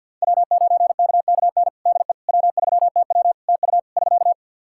Morse code, heard as a keyed tone, about 51 wpm, 700 Hz. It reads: W9KKN DE W3TW TU <SK>
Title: Practice QSO - Exchange 4 - Ending - 50wpm